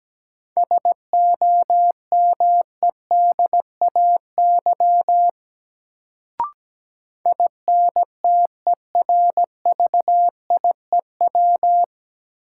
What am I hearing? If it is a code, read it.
Morse code, heard as a keyed tone, about 17 wpm, 700 Hz. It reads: SOMEDAY INTERVIEW
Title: Sentences from Top 1600 Words - New Word Drill Alt - 17wpm